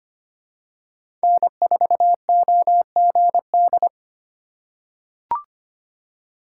Morse code, heard as a keyed tone, about 25 wpm, 700 Hz. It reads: N4OGD
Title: USA Call Signs - 5 Characters - 4x Character Spacing - 25wpm